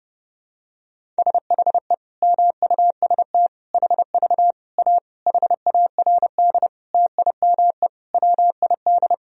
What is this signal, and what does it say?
Morse code, heard as a keyed tone, about 30 wpm, 700 Hz. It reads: SHE MUST HV A HARD TIME WID
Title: Sentences from Top 300 Words and Abbreviations 30wpm